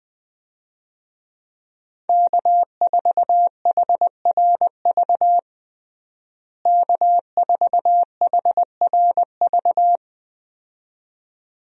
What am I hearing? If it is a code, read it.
Morse code, heard as a keyed tone, about 20 wpm, 700 Hz. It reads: K4HRV K4HRV
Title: USA Call Signs - 5 Characters - Repeated 2x - 20wpm